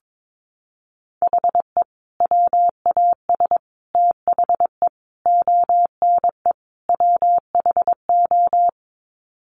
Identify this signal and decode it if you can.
Morse code, heard as a keyed tone, about 22 wpm, 700 Hz. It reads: HE WAS THE ONE WHO
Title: Sets of 5 Words Encore2 22wpm